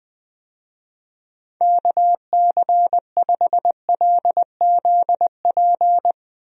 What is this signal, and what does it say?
Morse code, heard as a keyed tone, about 21 wpm, 700 Hz. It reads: KC5LZP
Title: USA Call Signs - 6 Characters - Repeated 2x - 20wpm